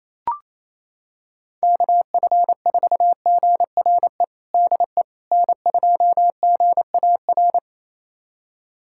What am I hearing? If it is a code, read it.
Morse code, heard as a keyed tone, about 28 wpm, 700 Hz. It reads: KF4GRE DE N2GAR
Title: USA Callsign DE Callsign 28wpm